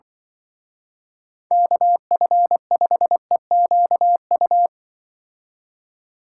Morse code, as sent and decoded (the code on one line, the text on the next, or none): -.- ..-. ..... --.- ..-
KF5EQU